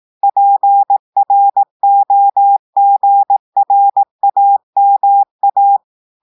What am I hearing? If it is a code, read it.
Morse code, heard as a keyed tone, about 18 wpm, 800 Hz. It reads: PROGRAMA
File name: portugais texte en clair 2 vitesse 18 mots minutes